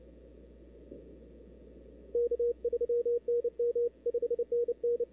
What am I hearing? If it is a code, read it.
Morse code, heard as a keyed tone, about 29 wpm, 470 Hz. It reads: K3NM 5NN
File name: CW / 2005-k5tr arrl cw dx -03:00